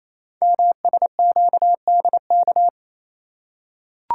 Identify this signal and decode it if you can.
Morse code, heard as a keyed tone, about 28 wpm, 700 Hz. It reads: MSQDK
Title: Code Groups - 5 Letters - 28wpm